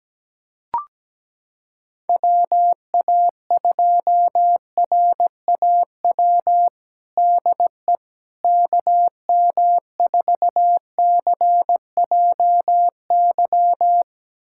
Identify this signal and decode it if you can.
Morse code, heard as a keyed tone, about 17 wpm, 700 Hz. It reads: WA2RAW DE KM4CJY
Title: USA Callsign DE Callsign - 6 Characters - 17wpm